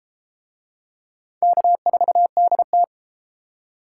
Morse code, heard as a keyed tone, about 33 wpm, 700 Hz. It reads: K4DT